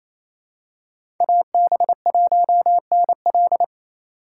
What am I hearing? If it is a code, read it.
Morse code, heard as a keyed tone, about 28 wpm, 700 Hz. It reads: AB1NL
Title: USA Call Signs - 3x Character Spacing - 28wpm